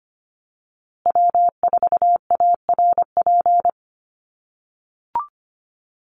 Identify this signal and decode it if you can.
Morse code, heard as a keyed tone, about 25 wpm, 700 Hz. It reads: W4ARP
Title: USA Call Signs Repeated 3x 25wpm